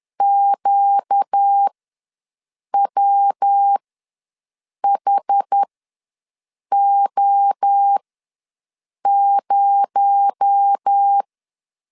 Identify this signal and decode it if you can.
Morse code, heard as a keyed tone, about 10 wpm, spaced as for 6 wpm, 800 Hz. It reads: QWHO0
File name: LEZIONI / 6 PKX0 / 6 PKX0 ws 10wpm 4220